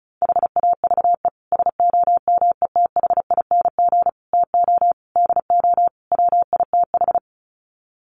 Morse code, heard as a keyed tone, about 34 wpm, 700 Hz. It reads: HAVE SOMETHING TO DO WITH
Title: Sets of 5 Words Encore 35wpm